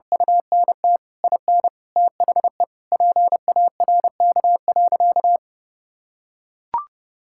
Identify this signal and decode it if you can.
Morse code, heard as a keyed tone, about 30 wpm, 700 Hz. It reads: UNT IN THE PARK.
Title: Sentences from Top 1600 Words 30wpm